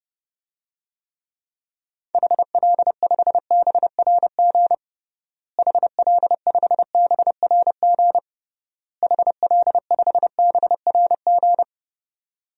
Morse code, heard as a keyed tone, about 32 wpm, 700 Hz. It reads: HL5BRG HL5BRG HL5BRG
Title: DXCC-100 Call Signs - 6 Characters - Repeated 3x - 30wpm